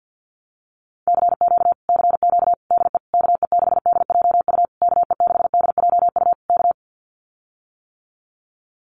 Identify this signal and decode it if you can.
Morse code, heard as a keyed tone, about 50 wpm, 700 Hz. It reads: CQ CQ DE KE6DJU KE6DJU K
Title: Common QSO Phrases Encore 50wpm